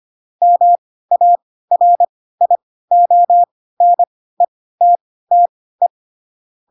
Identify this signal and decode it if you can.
Morse code, heard as a keyed tone, about 25 wpm, spaced as for 16 wpm, 700 Hz. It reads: MARIONETTE
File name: Pinocchio / Pinocchio ch7 10 25 ik1hge